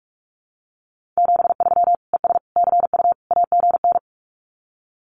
Morse code, heard as a keyed tone, about 46 wpm, 700 Hz. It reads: 73 ES CU AGN
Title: Common QSO Phrases Encore 45wpm